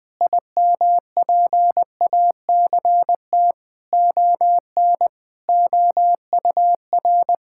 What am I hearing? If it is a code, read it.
Morse code, heard as a keyed tone, about 20 wpm, 700 Hz. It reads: IMPACT ON OUR
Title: Sentences from Top 1000 Words - Review Part 6 - 20wpm